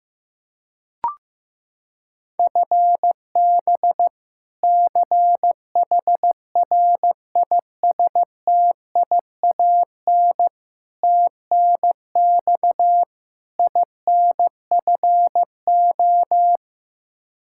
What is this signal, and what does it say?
Morse code, heard as a keyed tone, about 15 wpm, 700 Hz. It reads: FB CHRISTIAN TNX INFO
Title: K1USN SST QSO Practice 15wpm